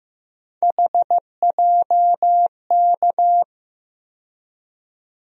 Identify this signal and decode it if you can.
Morse code, heard as a keyed tone, about 15 wpm, 700 Hz. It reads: HJK
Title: USA Call Sign Postfixes - 3 Letters 15wpm